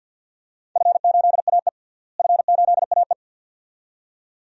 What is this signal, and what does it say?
Morse code, heard as a keyed tone, about 52 wpm, 700 Hz. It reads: F8AE F8AE
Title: DXCC-100 Call Signs - 4 Characters - Repeated 2x - 50wpm